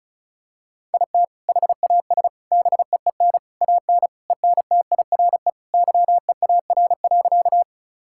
Morse code, heard as a keyed tone, about 34 wpm, 700 Hz. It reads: IT HAS BEEN AN ENTIRE YEAR.
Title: Sentences from Top 1000 Words - Review Part 5 - 35wpm